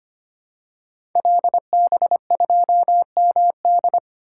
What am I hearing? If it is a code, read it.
Morse code, heard as a keyed tone, about 25 wpm, 700 Hz. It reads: LB2MD
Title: Code Groups - 5 Letter:Numbers - 25wpm